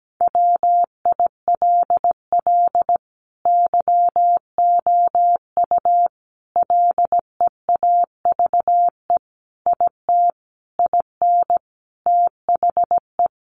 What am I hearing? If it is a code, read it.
Morse code, heard as a keyed tone, about 17 wpm, 700 Hz. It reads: WILL YOU LEAVE IT IN THE
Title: Sentences from Top 1300 Words 17wpm